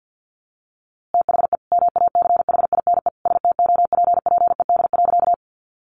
Morse code, heard as a keyed tone, about 50 wpm, 700 Hz. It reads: THE MACHINE STOPPED.
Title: A Sound Of Thunder 50wpm